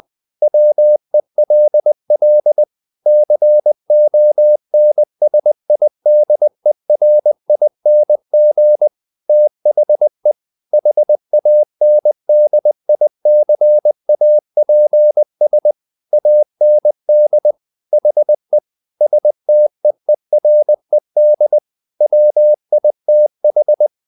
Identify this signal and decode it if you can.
Morse code, heard as a keyed tone, about 20 wpm, 600 Hz. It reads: WELL CONSIDERING THE HANDICAPS AND HE STEERED WITH
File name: the old man and the sea part 2 20wpm